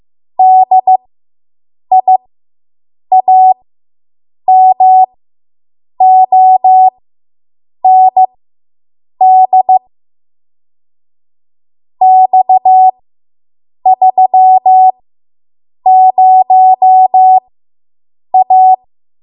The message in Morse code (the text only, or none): DIAMOND X30A